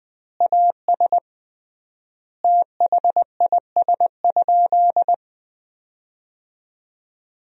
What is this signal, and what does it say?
Morse code, heard as a keyed tone, about 20 wpm, 700 Hz. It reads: AS THIS?
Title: 3x Word-Spacing - Sentences from Top 400 Words - 20wpm